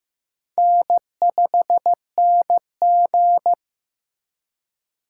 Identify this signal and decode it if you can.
Morse code, heard as a keyed tone, about 15 wpm, 700 Hz. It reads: N5NG
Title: USA Call Signs - 4x Character Spacing - 15wpm